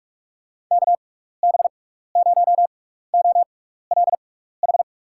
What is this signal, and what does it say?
Morse code, heard as a keyed tone, about 44 wpm, spaced as for 15 wpm, 700 Hz. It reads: KB0OLH
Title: USA Call Signs - 6x Character Spacing - 45wpm